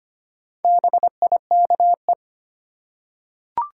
Morse code, heard as a keyed tone, about 25 wpm, 700 Hz. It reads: BIKE
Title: Four Letter Words - 25wpm